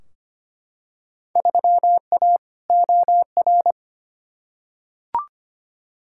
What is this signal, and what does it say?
Morse code, heard as a keyed tone, about 25 wpm, 700 Hz. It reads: 3A OR
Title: Field Day Exchanges - 25wpm